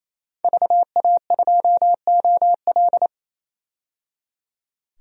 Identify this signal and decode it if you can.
Morse code, heard as a keyed tone, about 28 wpm, 700 Hz. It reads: VA2OL
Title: DXCC-100 Call Signs - 5 Characters - 28wpm